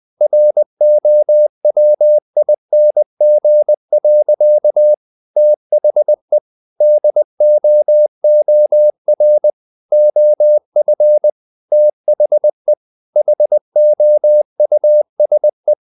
Morse code, heard as a keyed tone, about 20 wpm, 600 Hz. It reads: ROWING. THE DOOR OF THE HOUSE
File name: the old man and the sea / the old man and the sea part 1 20wpm